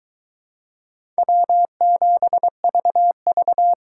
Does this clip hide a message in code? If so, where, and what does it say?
Yes: Morse code from 1.2 to 3.7 s, heard as a keyed tone, about 22 wpm, 700 Hz. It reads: W7VV